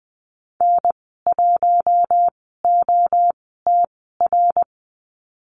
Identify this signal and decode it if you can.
Morse code, heard as a keyed tone, about 20 wpm, 700 Hz. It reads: N 1 O T R
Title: 076 Lesson 13g - Letter L - Call Signs 20wpm